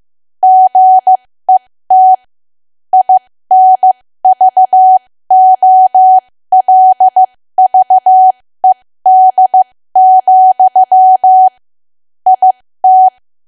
Morse code, heard as a keyed tone, about 15 wpm, spaced as for 13 wpm, 750 Hz. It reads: GET INVOLVED, IT
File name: cwa / W1AW-Level2 / 13WPM